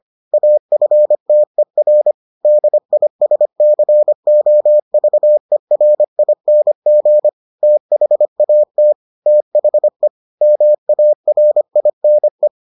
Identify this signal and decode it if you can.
Morse code, heard as a keyed tone, about 25 wpm, 600 Hz. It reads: AFTER DISCOVERING THAT THE MARINE